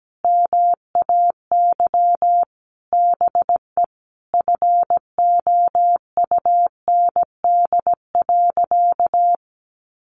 Morse code, heard as a keyed tone, about 17 wpm, 700 Hz. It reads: MAY BE FOUND.
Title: Sentences from Top 1500 Words 17wpm